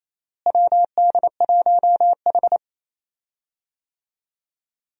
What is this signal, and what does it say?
Morse code, heard as a keyed tone, about 28 wpm, 700 Hz. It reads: WD1H
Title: USA Call Signs - 4 Characters - Repeated 2x - 28wpm